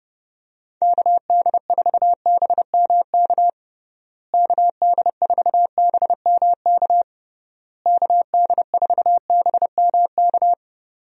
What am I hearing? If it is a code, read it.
Morse code, heard as a keyed tone, about 31 wpm, 700 Hz. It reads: KD4BMK KD4BMK KD4BMK
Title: USA Call Signs - 6 Characters - Repeated 3x - 30wpm